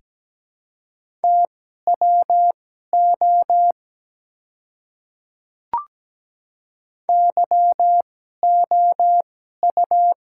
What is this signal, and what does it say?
Morse code, heard as a keyed tone, about 17 wpm, spaced as for 12 wpm, 700 Hz. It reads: TWO YOU
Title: 2x Character Spacing - Top 300 Words - 17wpm